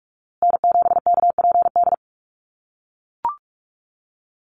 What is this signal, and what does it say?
Morse code, heard as a keyed tone, about 44 wpm, 700 Hz. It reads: N7KPD
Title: USA Call Signs Repeated 3x 45wpm